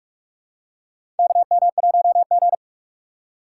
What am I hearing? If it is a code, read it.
Morse code, heard as a keyed tone, about 48 wpm, 700 Hz. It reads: KM1G
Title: USA Call Signs - 4 Characters - 2x Character Spacing - 45wpm